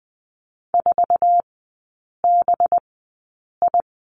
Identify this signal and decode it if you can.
Morse code, heard as a keyed tone, about 20 wpm, spaced as for 8 wpm, 700 Hz. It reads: 4BI
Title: 184 Lesson 30e - Letter b - Three Characters 20wpm